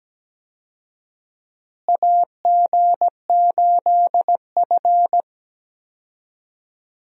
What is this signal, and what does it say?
Morse code, heard as a keyed tone, about 17 wpm, 700 Hz. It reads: AG8F
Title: USA Call Signs - 4 Characters - Repeated 3x - 17wpm